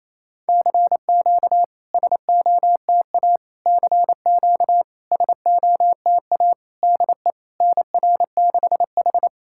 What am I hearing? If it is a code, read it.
Morse code, heard as a keyed tone, about 28 wpm, 700 Hz. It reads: CQ SOTA CQ SOTA DE NR6H
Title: SOTA QSO Practice 28wpm